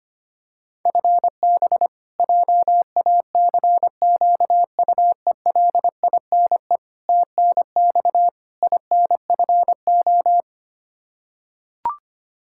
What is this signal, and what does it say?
Morse code, heard as a keyed tone, about 25 wpm, 700 Hz. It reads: FB JACQUELINE TNX INFO
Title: Practice QSO - Exchange 3 - More Info - 25wpm